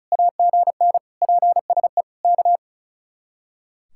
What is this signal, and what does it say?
Morse code, heard as a keyed tone, about 43 wpm, 700 Hz. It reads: AGN PSE K
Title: Speed-Racing Common QSO Phrases - 35wpm